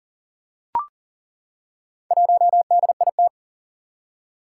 Morse code, heard as a keyed tone, about 40 wpm, 700 Hz. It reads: 1DIT
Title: Four Challenging Letter-Number Combinations 40wpm